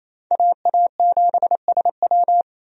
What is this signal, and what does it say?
Morse code, heard as a keyed tone, about 27 wpm, 700 Hz. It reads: AA7SW
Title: USA Call Signs - 5 Characters - 28wpm